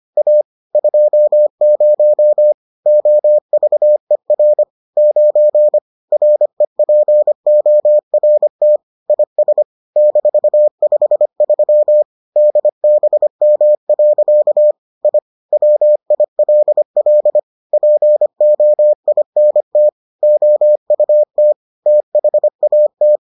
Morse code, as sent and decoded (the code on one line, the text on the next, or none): .- / ..--- ----- / --- ...- . .-. / ----. / .-. .--. --- .-. - / .. ... / -....- ..... ...-- / -.. -... -- .-.-.- / .. / .-- .. .-.. .-.. / .--. --- .. -. - / --- ..- - / - .... .- -
A 20 OVER 9 REPORT IS -53 DBM. I WILL POINT OUT THAT